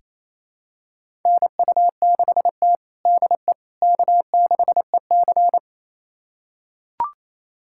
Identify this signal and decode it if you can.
Morse code, heard as a keyed tone, about 28 wpm, 700 Hz. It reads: NU6T DE K6EC
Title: USA Callsign DE Callsign - 4 Characters - 28wpm